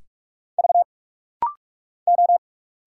Morse code, heard as a keyed tone, about 46 wpm, 700 Hz. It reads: V O
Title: Single Letters - Rapid-Fire 45wpm